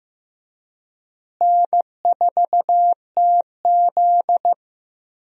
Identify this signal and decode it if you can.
Morse code, heard as a keyed tone, about 15 wpm, 700 Hz. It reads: N4TZ
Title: USA Call Signs - 4 Characters - Repeated 2x - 15wpm